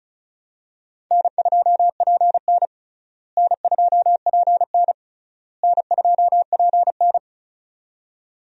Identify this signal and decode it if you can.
Morse code, heard as a keyed tone, about 34 wpm, 700 Hz. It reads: N2PN N2PN N2PN
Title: USA Call Signs - 4 Characters - Repeated 3x - 35wpm